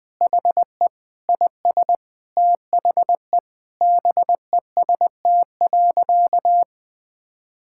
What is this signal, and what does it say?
Morse code, heard as a keyed tone, about 21 wpm, 700 Hz. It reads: HE IS THE BEST.
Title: Sentences with 3-Word Phrases - 20wpm